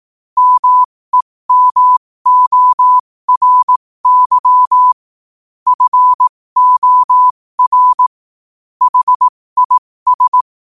Morse code, heard as a keyed tone, about 18 wpm, spaced as for 15 wpm, 1000 Hz. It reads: MEMORY FOR HIS
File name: qotdfiles / 15wpm